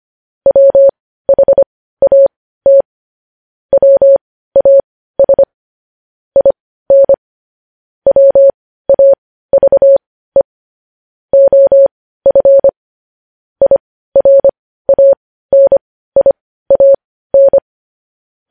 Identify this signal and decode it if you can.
Morse code, heard as a keyed tone, about 25 wpm, spaced as for 15 wpm, 550 Hz. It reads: WHAT WAS IN WAVE OF IRANIAN